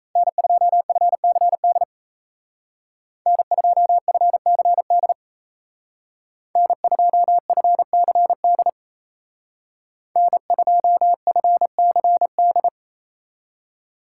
Morse code, heard as a keyed tone, about 35 wpm, 700 Hz. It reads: N2FCD N2FCD N2FCD N2FCD